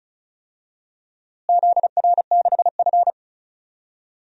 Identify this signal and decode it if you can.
Morse code, heard as a keyed tone, about 36 wpm, 700 Hz. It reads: ZR6F
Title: DXCC-100 Call Signs - 4 Characters - Repeated 3x - 35wpm